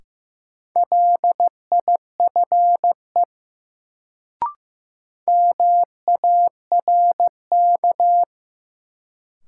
Morse code, heard as a keyed tone, about 15 wpm, 700 Hz. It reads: LIFE MARK
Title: Sentences from Top 300 Words - New Word Drill - 15wpm